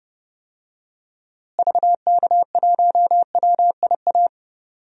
Morse code, heard as a keyed tone, about 30 wpm, 700 Hz. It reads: VK1WIA
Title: DXCC-100 Call Signs - 6 Characters - Repeated 3x - 30wpm